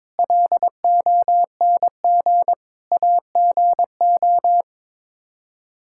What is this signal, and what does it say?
Morse code, heard as a keyed tone, about 22 wpm, 700 Hz. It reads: LONG AGO
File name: practice / 22wpm / Sets of 8 Words - 22wpm